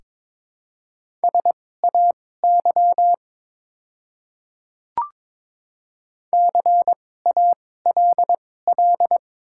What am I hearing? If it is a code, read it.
Morse code, heard as a keyed tone, about 22 wpm, spaced as for 16 wpm, 700 Hz. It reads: SAY CALL
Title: 2x Character Spacing - Top 200 Words - 22wpm